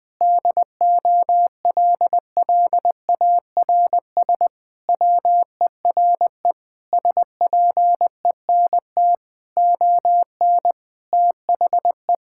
Morse code, heard as a keyed tone, about 20 wpm, 700 Hz. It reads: DOLLARS WERE SPENT ON THE
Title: Sentences with 5-Word Phrases - 20wpm